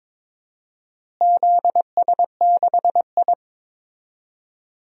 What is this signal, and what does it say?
Morse code, heard as a keyed tone, about 22 wpm, 700 Hz. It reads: ZS6I
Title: DXCC-100 Call Signs - 4 Characters - Repeated 2x - 22wpm